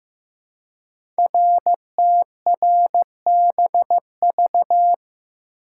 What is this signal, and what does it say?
Morse code, heard as a keyed tone, about 15 wpm, 700 Hz. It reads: RTRBV
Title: Code Groups - 5 Letters - 15wpm